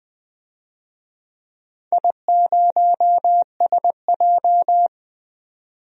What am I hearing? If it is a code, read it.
Morse code, heard as a keyed tone, about 20 wpm, 700 Hz. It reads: I0SJ
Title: DXCC-100 Call Signs - 4 Characters - 20wpm